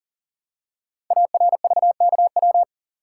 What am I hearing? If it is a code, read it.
Morse code, heard as a keyed tone, about 39 wpm, 700 Hz. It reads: ARVKW
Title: Code Groups - 5 Letters - 40wpm